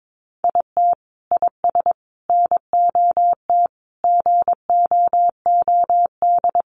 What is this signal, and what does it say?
Morse code, heard as a keyed tone, about 22 wpm, 700 Hz. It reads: IT IS NOT GOOD
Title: Sentences from Top 600 Words 22wpm